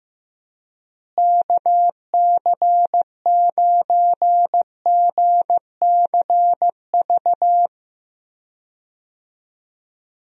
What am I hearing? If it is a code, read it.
Morse code, heard as a keyed tone, about 15 wpm, 700 Hz. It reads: KC9GCV
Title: USA Call Signs - 6 Characters - Repeated 2x - 15wpm